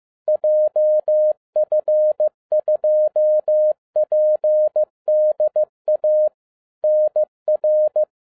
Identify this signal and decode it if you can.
Morse code, heard as a keyed tone, about 15 wpm, 610 Hz. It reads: JF2PDA NR